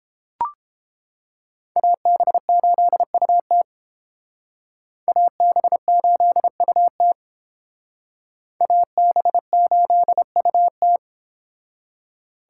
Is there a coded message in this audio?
Yes: Morse code from 1.8 to 11.0 s, heard as a keyed tone, about 32 wpm, 700 Hz. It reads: AB8UT AB8UT AB8UT